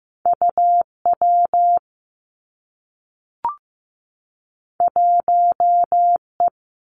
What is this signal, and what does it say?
Morse code, heard as a keyed tone, about 15 wpm, 700 Hz. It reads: U W 1 E
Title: Two Challenging Letter-Number Combinations 15wpm